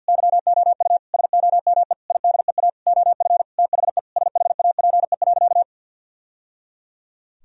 Morse code, heard as a keyed tone, about 50 wpm, 700 Hz. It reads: YOU SOME IDEA OF THE SHAPE.
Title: Sentences from Top 1000 Words - Review Part 3 - 50wpm